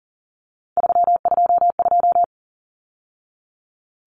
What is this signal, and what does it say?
Morse code, heard as a keyed tone, about 40 wpm, 700 Hz. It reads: 322
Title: Numbers - 3-Digit 40wpm